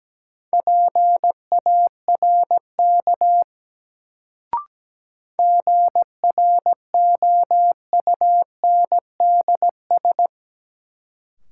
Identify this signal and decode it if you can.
Morse code, heard as a keyed tone, about 17 wpm, 700 Hz. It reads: PARK GROUNDS
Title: Sentences from Top 1300 Words - New Word Drill Alt - 17wpm